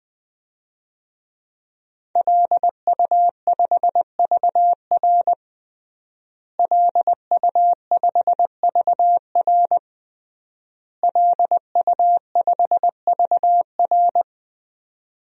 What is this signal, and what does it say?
Morse code, heard as a keyed tone, about 21 wpm, 700 Hz. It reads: LU5VR LU5VR LU5VR
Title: DXCC-100 Call Signs - 5 Characters - Repeated 3x - 20wpm